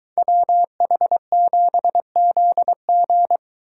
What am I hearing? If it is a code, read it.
Morse code, heard as a keyed tone, about 23 wpm, 700 Hz. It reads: WH7ZG